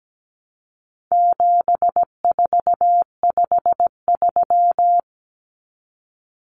Morse code, heard as a keyed tone, about 17 wpm, 700 Hz. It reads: 7453
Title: Numbers - 4-Digit 17wpm